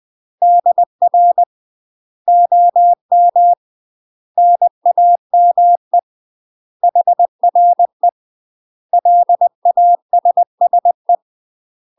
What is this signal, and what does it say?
Morse code, heard as a keyed tone, about 20 wpm, 700 Hz. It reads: DR OM NAME HRE LASSE
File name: UK MASTER QSOs TRAINING / 20WPM / UK MASTER QSOs 0001